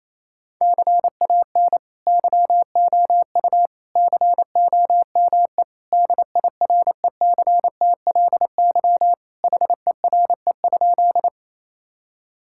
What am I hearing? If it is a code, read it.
Morse code, heard as a keyed tone, about 27 wpm, 700 Hz. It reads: CAN YOU COME DIRECTLY HERE?
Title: Sentences from Top 1000 Words - Review Part 5 - 28wpm